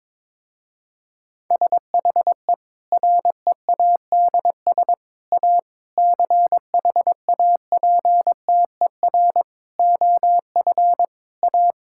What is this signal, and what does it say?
Morse code, heard as a keyed tone, about 22 wpm, 700 Hz. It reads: SHE READS A CHAPTER OF A
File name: Difficult Sentences with Hints - 22wpm